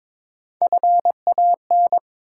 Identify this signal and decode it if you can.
Morse code, heard as a keyed tone, about 22 wpm, 700 Hz. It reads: FAN
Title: Sentences from Top 1300 Words - New Word Drill Alt - 22wpm